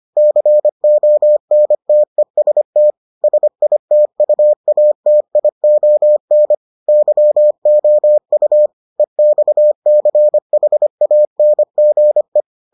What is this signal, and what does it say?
Morse code, heard as a keyed tone, about 25 wpm, 600 Hz. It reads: CONTEST SITUATION YOU EXCHANGE